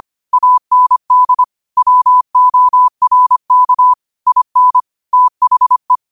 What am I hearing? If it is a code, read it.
Morse code, heard as a keyed tone, about 25 wpm, 1000 Hz. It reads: AND WORK IN THE
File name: qotdfiles / 25wpm